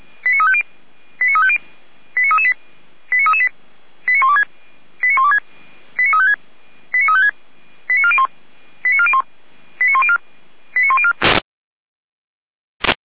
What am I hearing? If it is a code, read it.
Five-tone selective call (ZVEI), heard as quick runs of five tones, tones of about 70 ms. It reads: 89350 89350 89338 89338 89126 89126 89356 89356 89401 89401 89104 89104